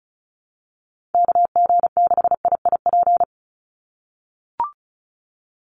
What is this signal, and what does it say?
Morse code, heard as a keyed tone, about 34 wpm, 700 Hz. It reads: KG6IIP
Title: Call Signs - 35wpm